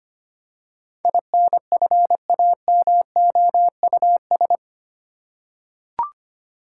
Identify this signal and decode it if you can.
Morse code, heard as a keyed tone, about 25 wpm, 700 Hz. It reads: INFAMOUS
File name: Eight Letter Words - 25wpm